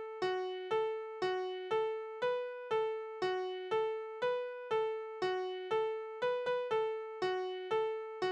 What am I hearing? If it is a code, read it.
Busy tone (France), 440 Hz, with noise.